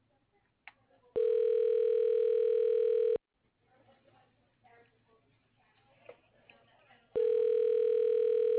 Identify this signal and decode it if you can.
Ringback tone (North America), 440 + 480 Hz.